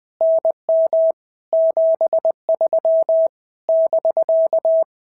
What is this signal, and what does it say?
Morse code, heard as a keyed tone, about 20 wpm, 650 Hz. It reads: NM 73 <BK>
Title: POTA QSO Practice - 20wpm